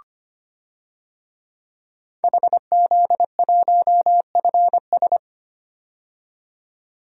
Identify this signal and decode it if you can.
Morse code, heard as a keyed tone, about 25 wpm, 700 Hz. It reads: HZ1FS
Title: DXCC-100 Call Signs - 5 Characters - 25wpm